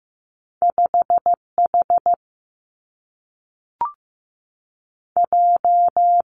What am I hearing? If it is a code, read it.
Morse code, heard as a keyed tone, about 15 wpm, 700 Hz. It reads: 5 H J
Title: Two Challenging Letter-Number Combinations 15wpm